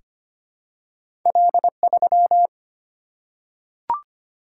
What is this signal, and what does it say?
Morse code, heard as a keyed tone, about 25 wpm, 700 Hz. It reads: L3
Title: Single-Letter Single-Number 25wpm